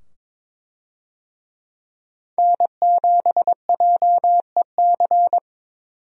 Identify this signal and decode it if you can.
Morse code, heard as a keyed tone, about 22 wpm, 700 Hz. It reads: N7JEC
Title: USA Call Signs - 5 Characters - 22wpm